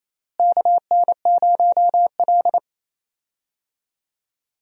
Morse code, heard as a keyed tone, about 28 wpm, 700 Hz. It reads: KN0L
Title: USA Call Signs - 4 Characters - Repeated 3x - 28wpm